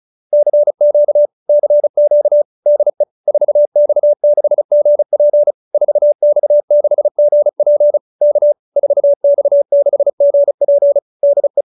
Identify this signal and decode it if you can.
Morse code, heard as a keyed tone, about 35 wpm, 590 Hz. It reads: CQ CQ DE 4X6GP 4X6GP K 4X6GP DE